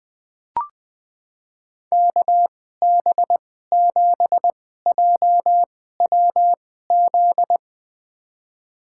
Morse code, heard as a keyed tone, about 20 wpm, spaced as for 14 wpm, 700 Hz. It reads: KB7JWZ
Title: USA Call Signs - 6 Characters - 2x Character Spacing - 20wpm